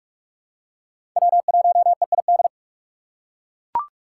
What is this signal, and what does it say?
Morse code, heard as a keyed tone, about 41 wpm, 700 Hz. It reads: W1EID
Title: USA Call Signs - 5 Characters - 3x Character Spacing - 45wpm